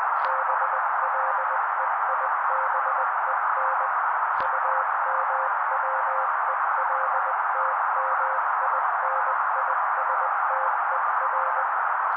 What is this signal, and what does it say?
Morse code, heard as a keyed tone, about 20 wpm, 530 Hz. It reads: BLEIBEN UMWELTMINISTER
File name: daily / 18wpm noise